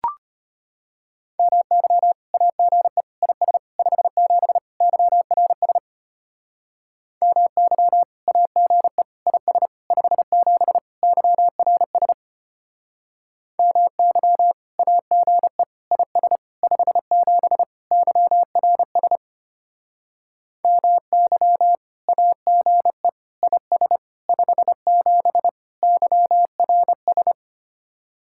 Morse code, heard as a keyed tone, about 31 wpm, 700 Hz. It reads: MY AGE IS 57 YRS MY AGE IS 57 YRS MY AGE IS 57 YRS MY AGE IS 57 YRS
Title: Speed-Racing Common QSO Phrases Encore - 25wpm